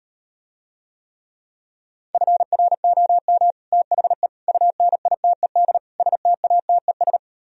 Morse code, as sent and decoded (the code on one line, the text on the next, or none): ..-. .-. --- -- / - .... / ..- -. .. - . -.. / ... - .- - . ...
FROM THE UNITED STATES